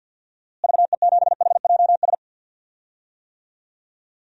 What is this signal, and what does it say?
Morse code, heard as a keyed tone, about 46 wpm, 700 Hz. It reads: VE7HJS
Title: DXCC-100 Call Signs - 6 Characters - Repeated 2x - 50wpm